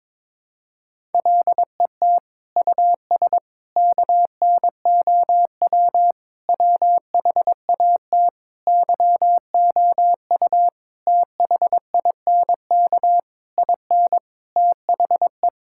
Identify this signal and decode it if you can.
Morse code, heard as a keyed tone, about 22 wpm, 700 Hz. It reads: LET US KNOW WHAT YOU THINK IN THE
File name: Sets of 9 Words - 22wpm